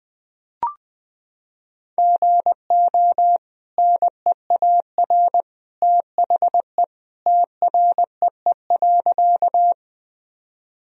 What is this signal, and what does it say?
Morse code, heard as a keyed tone, about 20 wpm, 700 Hz. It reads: GO NEAR THE TREE.
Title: Sentences from Top 1000 Words - Review Part 1 - 20wpm